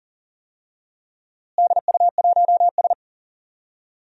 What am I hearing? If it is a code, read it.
Morse code, heard as a keyed tone, about 40 wpm, 700 Hz. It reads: DU1S
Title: DXCC-100 Call Signs - 4 Characters - Repeated 2x - 40wpm